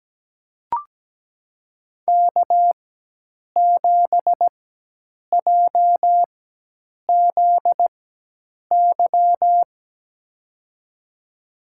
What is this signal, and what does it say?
Morse code, heard as a keyed tone, about 17 wpm, spaced as for 8 wpm, 700 Hz. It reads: K7JZY
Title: USA Call Signs - 4x Character Spacing - 17wpm